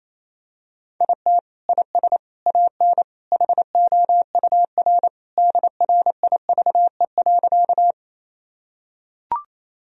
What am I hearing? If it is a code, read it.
Morse code, heard as a keyed tone, about 28 wpm, 700 Hz. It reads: IT IS AN HOUR DRIVE.
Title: Sentences from Top 1000 Words - Review Part 2 - 28wpm